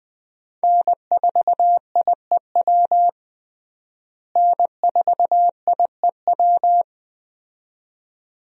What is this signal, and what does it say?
Morse code, heard as a keyed tone, about 20 wpm, 700 Hz. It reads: N4IEW N4IEW
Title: USA Call Signs - 5 Characters - Repeated 2x - 20wpm